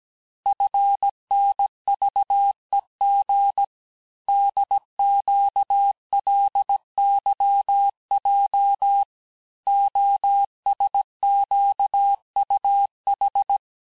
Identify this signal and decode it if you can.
Morse code, heard as a keyed tone, about 17 wpm, 800 Hz. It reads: FNVEG DQLYJ OSQUH
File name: corrige-cours-1581-texte-chiffre-vitesse-16-mpm